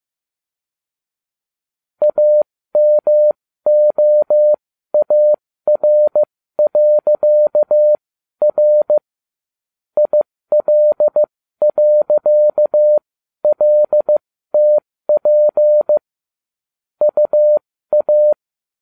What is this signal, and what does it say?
Morse code, heard as a keyed tone, about 15 wpm, spaced as for 12 wpm, 600 Hz. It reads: AMOAR.R IL.LTP UA
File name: cw / ubacw12 2